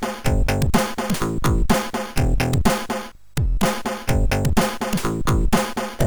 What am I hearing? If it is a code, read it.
There is no code.